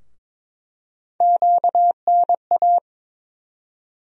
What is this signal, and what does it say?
Morse code, heard as a keyed tone, about 22 wpm, 700 Hz. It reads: QNA
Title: USA Call Sign Postfixes - 3 Letters 22wpm